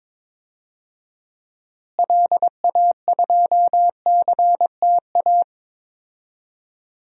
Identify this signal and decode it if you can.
Morse code, heard as a keyed tone, about 22 wpm, 700 Hz. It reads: LA2CTA
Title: DXCC-100 Call Signs - 6 Characters - Repeated 2x - 22wpm